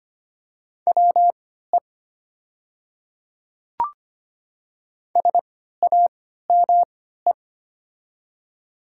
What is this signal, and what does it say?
Morse code, heard as a keyed tone, about 25 wpm, spaced as for 14 wpm, 700 Hz. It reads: WE SAME
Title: 3x Character Spacing - Top 300 Words - 25wpm